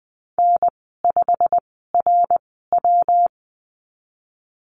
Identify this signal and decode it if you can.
Morse code, heard as a keyed tone, about 20 wpm, spaced as for 15 wpm, 700 Hz. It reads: N5RW
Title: 128 Lesson 21g - Number 6 - Call Signs 20wpm